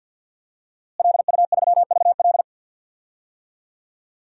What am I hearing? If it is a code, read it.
Morse code, heard as a keyed tone, about 51 wpm, 700 Hz. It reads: LU3VL
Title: DXCC-100 Call Signs - 5 Characters - Repeated 2x - 50wpm